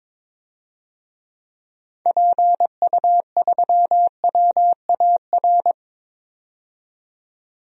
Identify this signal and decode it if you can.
Morse code, heard as a keyed tone, about 22 wpm, 700 Hz. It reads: PU3WAR
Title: DXCC-100 Call Signs - 6 Characters - 22wpm